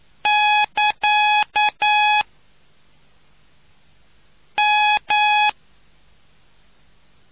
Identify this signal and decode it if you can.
Morse code, heard as a keyed tone, about 9 wpm, 840 Hz. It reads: <KA> M